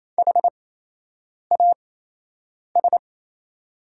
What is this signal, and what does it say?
Morse code, heard as a keyed tone, about 28 wpm, spaced as for 8 wpm, 700 Hz. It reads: HAS
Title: 8x Character Spacing - Top 200 Words - 28wpm